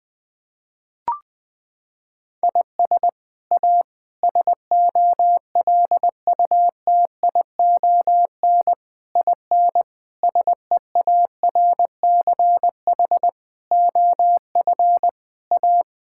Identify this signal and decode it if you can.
Morse code, heard as a keyed tone, about 21 wpm, 700 Hz. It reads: IS A SOLUTION IN SEARCH OF A
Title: Sets of 8 Words - 20wpm